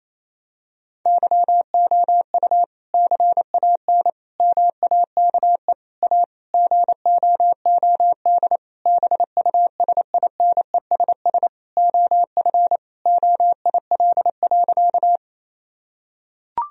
Morse code, heard as a keyed tone, about 28 wpm, 700 Hz. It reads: YOU CAN MAKE A GOOD BUSINESS OF OIL.
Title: Sentences from Top 1000 Words - Review Part 4 - 28wpm